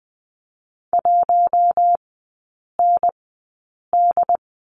Morse code, heard as a keyed tone, about 20 wpm, spaced as for 8 wpm, 700 Hz. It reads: 1ND
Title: 127 Lesson 21e - Number 6 - Three Characters 20wpm